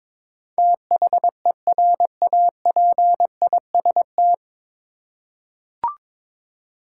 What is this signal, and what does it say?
Morse code, heard as a keyed tone, about 22 wpm, 700 Hz. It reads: THERAPIST
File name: 9 Letter Words - First 3 Letters Given - 22wpm